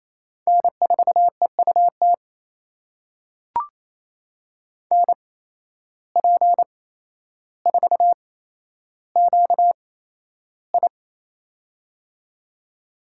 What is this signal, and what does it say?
Morse code, heard as a keyed tone, about 27 wpm, spaced as for 12 wpm, 700 Hz. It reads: N4EUT NP4QI
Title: USA Call Signs - 5 Characters - 8x Character Spacing - 28wpm